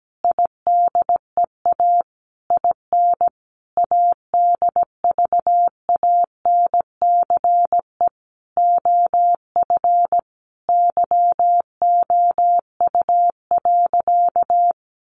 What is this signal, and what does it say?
Morse code, heard as a keyed tone, about 17 wpm, 700 Hz. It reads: IDEA IN ADVANCE OF YOU.
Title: Sentences from Top 1400 Words 17wpm